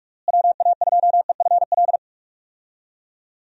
Morse code, heard as a keyed tone, about 46 wpm, 700 Hz. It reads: WA2EFL
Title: USA Call Signs - 6 Characters - Repeated 3x - 45wpm